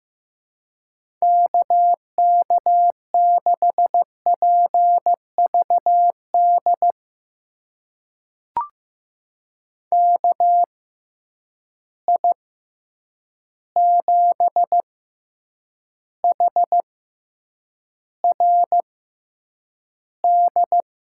Morse code, heard as a keyed tone, about 15 wpm, spaced as for 8 wpm, 700 Hz. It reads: KK6PVD KI7HRD
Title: USA Call Signs - 6x Character Spacing - 15wpm